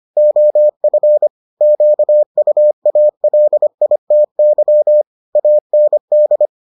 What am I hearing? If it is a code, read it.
Morse code, heard as a keyed tone, about 25 wpm, 600 Hz. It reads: OF QUALITY AND